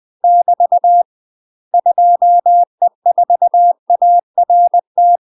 Morse code, heard as a keyed tone, about 20 wpm, 700 Hz. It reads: <BT> 2E4ART
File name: UK MASTER QSOs TRAINING / 20WPM / UK MASTER QSOs 0063